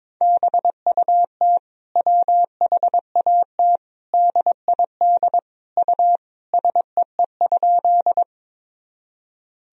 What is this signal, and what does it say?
Morse code, heard as a keyed tone, about 21 wpm, 700 Hz. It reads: BUT WHAT DID U SEE?
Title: Sentences from Top 100 Words and Abbreviations 22wpm